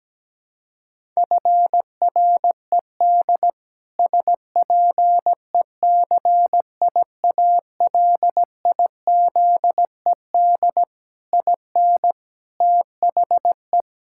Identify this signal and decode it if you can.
Morse code, heard as a keyed tone, about 17 wpm, 700 Hz. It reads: FRED SPECIALIZED IN THE
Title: Pangram Challenge v2 - 17wpm